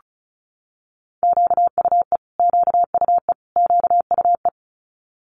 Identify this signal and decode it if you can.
Morse code, heard as a keyed tone, about 36 wpm, 700 Hz. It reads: QUE QUE QUE
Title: Multi-letter Phonemes Repeated 3x 35wpm